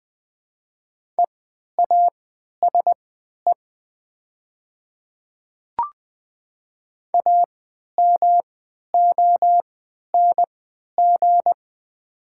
Morse code, heard as a keyed tone, about 20 wpm, spaced as for 11 wpm, 690 Hz. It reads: EASE AMONG
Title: 3x Character Spacing - Top 500 Words - 20wpm